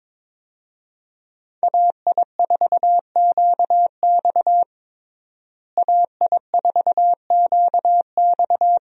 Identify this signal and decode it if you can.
Morse code, heard as a keyed tone, about 22 wpm, 700 Hz. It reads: AI4QX AI4QX
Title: USA Call Signs - 5 Characters - Repeated 2x - 22wpm